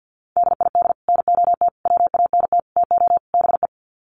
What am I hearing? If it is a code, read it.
Morse code, heard as a keyed tone, about 50 wpm, 700 Hz. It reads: DID NOT WANT TO BE
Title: Sets of 5 Words Encore 50wpm